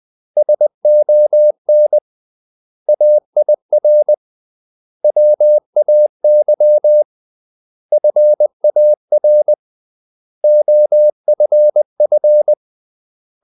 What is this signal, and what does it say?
Morse code, heard as a keyed tone, about 20 wpm, 600 Hz. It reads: SON AIR WAY FAR OFF
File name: CW / Practice / 500MostCommonEnglishWords20